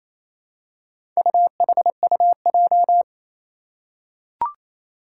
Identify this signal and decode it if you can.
Morse code, heard as a keyed tone, about 28 wpm, 700 Hz. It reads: UHUJ
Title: Four Challenging Letter-Number Combinations 28wpm